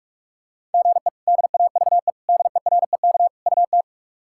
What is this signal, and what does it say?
Morse code, heard as a keyed tone, about 41 wpm, 700 Hz. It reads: GE DAVE DEREK UT
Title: K1USN SST QSO Practice - 45wpm